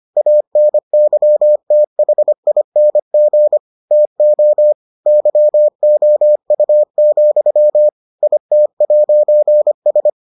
Morse code, heard as a keyed tone, about 25 wpm, 600 Hz. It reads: ANYTHING TO YOU, IT'S